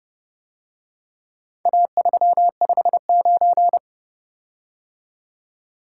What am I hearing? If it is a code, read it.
Morse code, heard as a keyed tone, about 29 wpm, 700 Hz. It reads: A359